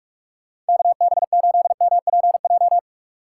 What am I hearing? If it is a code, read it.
Morse code, heard as a keyed tone, about 56 wpm, 700 Hz. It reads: KB8MPJ
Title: Speed Racing - USA 2x3 Call Signs 45wpm